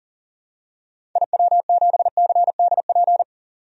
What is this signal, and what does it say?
Morse code, heard as a keyed tone, about 40 wpm, 700 Hz. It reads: IW7CDP
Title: DXCC-100 Call Signs - 6 Characters - 40wpm